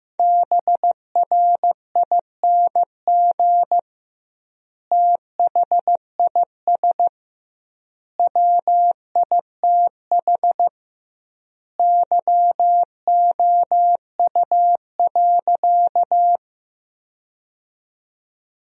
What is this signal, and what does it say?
Morse code, heard as a keyed tone, about 16 wpm, 700 Hz. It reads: BRING THIS WITH YOU.
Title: 2x Word-Spacing - Sentences from Top 500 Words - 15wpm